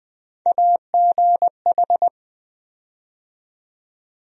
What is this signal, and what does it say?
Morse code, heard as a keyed tone, about 20 wpm, 700 Hz. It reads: AGH
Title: USA Call Sign Postfixes - 3 Letters 20wpm